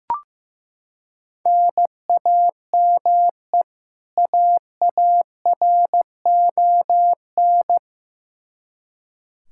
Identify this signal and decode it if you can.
Morse code, heard as a keyed tone, about 15 wpm, 700 Hz. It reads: NAME AARON
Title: Practice QSO - Exchange 1 - Beginning - 15wpm